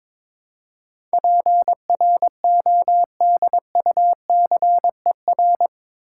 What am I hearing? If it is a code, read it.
Morse code, heard as a keyed tone, about 22 wpm, 700 Hz. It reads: PRODUCER
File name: practice / 22wpm / Eight Letter Words - 22wpm